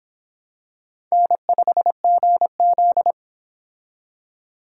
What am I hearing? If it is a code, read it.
Morse code, heard as a keyed tone, about 26 wpm, 700 Hz. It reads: N5GZ